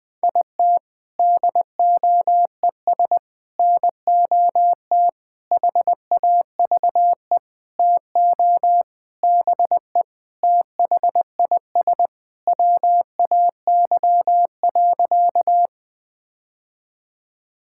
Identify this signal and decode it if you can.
Morse code, heard as a keyed tone, about 19 wpm, 700 Hz. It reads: IT DOES NOT HAVE TO BE THIS WAY.
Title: Sentences with 5-Word Phrases - Spoken Beginning - 20wpm